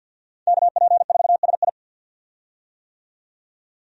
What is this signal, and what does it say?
Morse code, heard as a keyed tone, about 48 wpm, 700 Hz. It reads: KP4SI
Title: USA Call Signs - 5 Characters - 50wpm